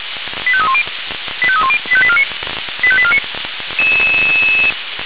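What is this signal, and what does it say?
Five-tone selective call (ZVEI), heard as quick runs of five tones, tones of about 70 ms. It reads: 85310 85310 85849 85849
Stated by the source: ZVEI five-tone call 85840 received by SDR.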